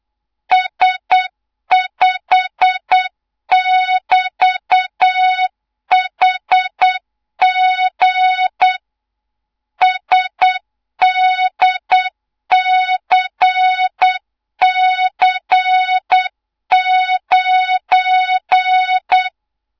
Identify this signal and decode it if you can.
Morse code, heard as a keyed tone, about 8 wpm, 3700 Hz, with noise. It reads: S5<BT>HG SDCC9
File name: Morze-HA2MN / hang / 24 GYAKORLAT